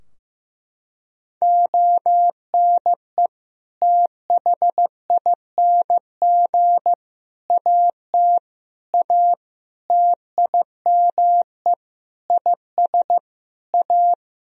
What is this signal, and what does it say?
Morse code, heard as a keyed tone, about 15 wpm, 700 Hz. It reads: ONE THING AT A TIME IS A
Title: Sentences from Top 1000 Words - Review Part 2 - 15wpm